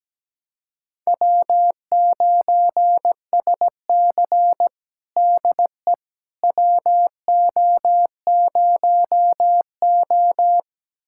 Morse code, heard as a keyed tone, about 17 wpm, 700 Hz. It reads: W9SC DE WO0O
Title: USA Callsign DE Callsign - 4 Characters - 17wpm